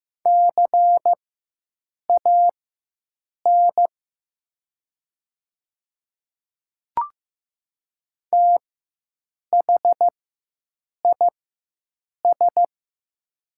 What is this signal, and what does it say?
Morse code, heard as a keyed tone, about 15 wpm, spaced as for 7 wpm, 700 Hz. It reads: CAN THIS
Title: 4x Character Spacing - Top 200 Words - 15wpm